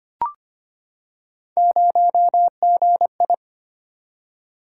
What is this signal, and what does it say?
Morse code, heard as a keyed tone, about 25 wpm, 700 Hz. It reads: 0GI
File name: Three Letter-Number Combinations - 25wpm